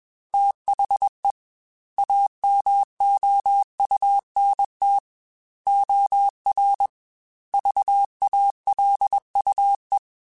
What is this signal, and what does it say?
Morse code, heard as a keyed tone, about 21 wpm, 800 Hz. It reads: THE AMOUNT OR VALUE
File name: Technical Subjects / Technical Subjects-mp3s / XC @20